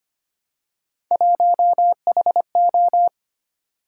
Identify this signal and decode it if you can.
Morse code, heard as a keyed tone, about 25 wpm, 700 Hz. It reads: 1HO
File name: Three Letter-Number Combinations - 25wpm